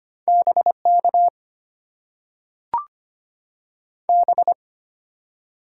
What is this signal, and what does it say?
Morse code, heard as a keyed tone, about 25 wpm, 700 Hz. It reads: BK B
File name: Top 90 QSO Elements - 8x Character Spacing - 25wpm